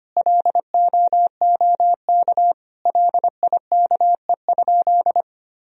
Morse code, heard as a keyed tone, about 25 wpm, 700 Hz. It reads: LOOK LIKE?
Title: Sentences from Top 100 Words and Abbreviations 25wpm